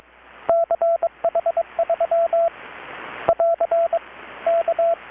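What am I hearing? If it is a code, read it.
Morse code, heard as a keyed tone, about 21 wpm, 670 Hz, with radio noise. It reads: CH3 <AR> K
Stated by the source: Morse from KPH's Night of Nights XX maritime radiotelegraphy event, received in Chiapas, Mexico about 3,800 km away on a Kenwood R-600 with a grounded 10 m long-wire antenna.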